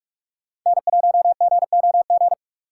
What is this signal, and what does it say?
Morse code, heard as a keyed tone, about 42 wpm, 700 Hz. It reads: N1GOG
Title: USA Call Signs - 5 Characters - 3x Character Spacing - 45wpm